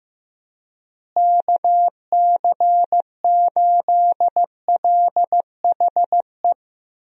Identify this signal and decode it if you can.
Morse code, heard as a keyed tone, about 15 wpm, 700 Hz. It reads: KC8LHE
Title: USA Call Signs - 6 Characters - Repeated 2x - 15wpm